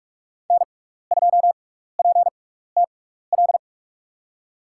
Morse code, heard as a keyed tone, about 44 wpm, spaced as for 15 wpm, 690 Hz. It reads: N2PTL
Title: USA Call Signs - 5 Characters - 6x Character Spacing - 45wpm